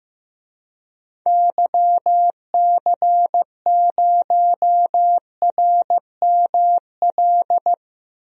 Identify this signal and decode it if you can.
Morse code, heard as a keyed tone, about 16 wpm, 700 Hz. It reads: YC0RML
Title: DXCC-100 Call Signs - 6 Characters - Repeated 3x - 15wpm